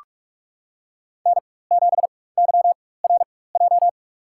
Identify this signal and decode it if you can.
Morse code, heard as a keyed tone, about 41 wpm, spaced as for 20 wpm, 700 Hz. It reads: N7YRJ
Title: USA Call Signs - 5 Characters - 4x Character Spacing - 45wpm